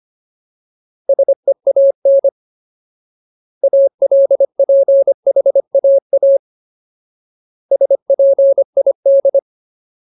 Morse code, heard as a keyed tone, about 25 wpm, 550 Hz. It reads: SEAN ALPHAA SPID
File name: Ultimate Word List / 25WPM Ultimate